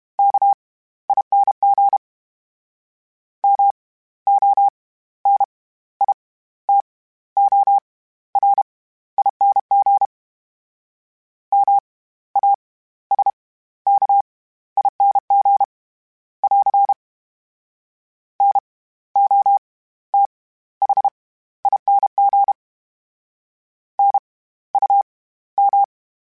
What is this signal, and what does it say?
Morse code, heard as a keyed tone, about 32 wpm, spaced as for 15 wpm, 800 Hz. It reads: KING MONITORING MASKING <AR> NOTHING NUM
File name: Special INGs / Special INGs-mp3s / SlowToFaster ing @30